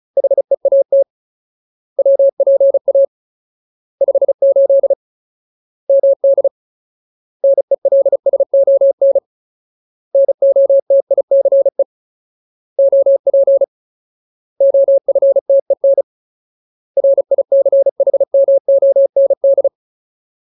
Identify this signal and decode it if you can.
Morse code, heard as a keyed tone, about 35 wpm, 550 Hz. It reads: HEAT WPA 58 MD NELSON NOTICE OP OFTEN RICHMOND
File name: Ultimate Word List / 35WPM Ultimate